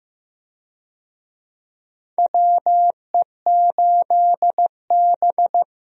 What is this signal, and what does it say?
Morse code, heard as a keyed tone, about 15 wpm, 700 Hz. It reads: WE8B
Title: USA Call Signs - 4 Characters - Repeated 2x - 15wpm